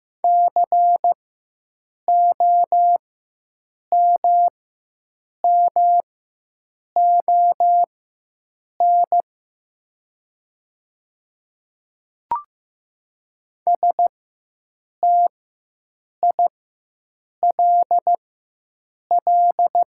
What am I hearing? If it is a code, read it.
Morse code, heard as a keyed tone, about 15 wpm, spaced as for 7 wpm, 700 Hz. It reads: COMMON STILL
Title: 4x Character Spacing - Top 500 Words - 15wpm